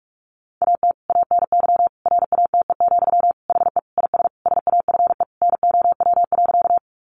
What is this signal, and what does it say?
Morse code, heard as a keyed tone, about 44 wpm, 700 Hz. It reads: AT ANY RATE, HE IS SAFE NOW.
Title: Sentences from Top 900 Words 45wpm